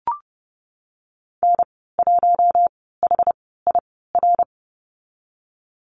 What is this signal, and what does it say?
Morse code, heard as a keyed tone, about 30 wpm, spaced as for 17 wpm, 700 Hz. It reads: N1HIR
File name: Lesson 19g - Letter w - Call Signs 30wpm